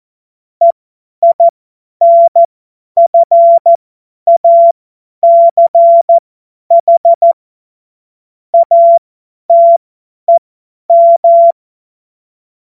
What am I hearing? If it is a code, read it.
Morse code, heard as a keyed tone, about 14 wpm, spaced as for 10 wpm, 680 Hz. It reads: EINFACH ATEM
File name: Morsekurs / Witze / Witz 31. August 2021, 50 F70